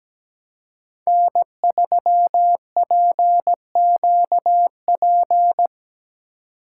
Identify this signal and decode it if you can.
Morse code, heard as a keyed tone, about 20 wpm, 700 Hz. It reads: N3PQP
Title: Speed Racing - USA Call Signs 17wpm